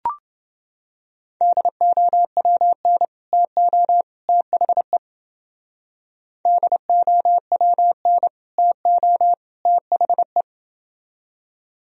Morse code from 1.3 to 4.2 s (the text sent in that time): DOWN TO